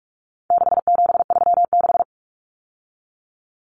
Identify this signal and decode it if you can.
Morse code, heard as a keyed tone, about 44 wpm, 700 Hz. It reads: 6736
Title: Common Cut Numbers - 4-Digit 45wpm